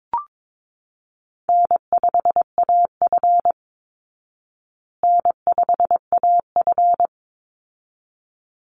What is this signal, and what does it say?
Morse code, heard as a keyed tone, about 22 wpm, 690 Hz. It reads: N5AF N5AF
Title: Easy Call Signs Repeated 2x - 22wpm